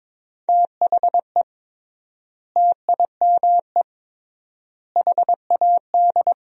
Morse code, heard as a keyed tone, about 23 wpm, 700 Hz. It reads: THE TIME HAD
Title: 3x Word-Spacing - Sentences from Top 100 Words - 22wpm